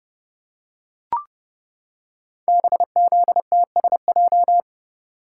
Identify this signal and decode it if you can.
Morse code, heard as a keyed tone, about 30 wpm, 700 Hz. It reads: BZTSJ
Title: Code Groups - 5 Letters - 30wpm